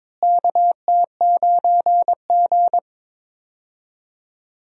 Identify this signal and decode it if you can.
Morse code, heard as a keyed tone, about 22 wpm, 700 Hz. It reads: KT9G
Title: USA Call Signs - 4 Characters - Repeated 2x - 22wpm